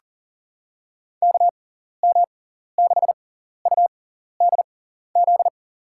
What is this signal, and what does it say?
Morse code, heard as a keyed tone, about 40 wpm, spaced as for 14 wpm, 700 Hz. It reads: KM6UDZ
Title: USA Call Signs - 6 Characters - 6x Character Spacing - 40wpm